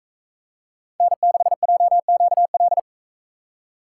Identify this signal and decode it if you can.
Morse code, heard as a keyed tone, about 39 wpm, 700 Hz. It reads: N6JQL